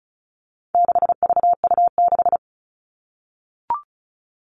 Morse code, heard as a keyed tone, about 34 wpm, 700 Hz. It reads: 6VU6
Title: All Cut Numbers - 4-Digit 35wpm